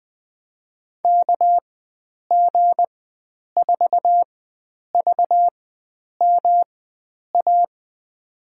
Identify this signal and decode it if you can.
Morse code, heard as a keyed tone, about 20 wpm, spaced as for 9 wpm, 700 Hz. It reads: KG4VMA
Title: USA Call Signs - 6 Characters - 4x Character Spacing - 20wpm